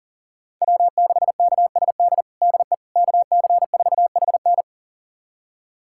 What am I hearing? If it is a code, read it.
Morse code, heard as a keyed tone, about 37 wpm, 700 Hz. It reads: W6KSD DE KC4HN
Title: USA Callsign DE Callsign - 5 Characters - 40wpm